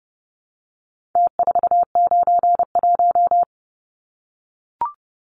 Morse code, heard as a keyed tone, about 29 wpm, 700 Hz. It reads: T491
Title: Numbers - Contest Style - 4-Digit 30wpm